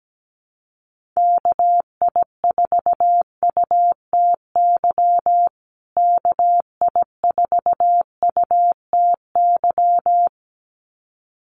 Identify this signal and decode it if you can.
Morse code, heard as a keyed tone, about 17 wpm, 700 Hz. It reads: KI4UTY KI4UTY
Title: USA Call Signs Repeated 2x 17wpm